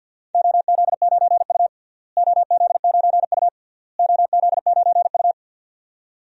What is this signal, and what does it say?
Morse code, heard as a keyed tone, about 52 wpm, 700 Hz. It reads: OZ9U OZ9U OZ9U
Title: DXCC-100 Call Signs - 4 Characters - Repeated 3x - 50wpm